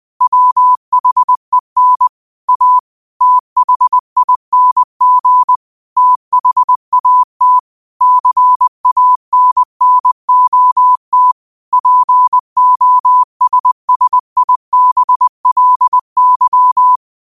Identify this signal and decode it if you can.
Morse code, heard as a keyed tone, about 20 wpm, 1000 Hz. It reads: WHEN A THING THAT CANNOT POSSIBLY
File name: qotdfiles / 20wpm